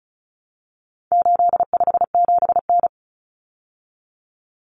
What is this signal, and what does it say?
Morse code, heard as a keyed tone, about 34 wpm, 700 Hz. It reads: 857N
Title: Common Cut Numbers - 4-Digit 35wpm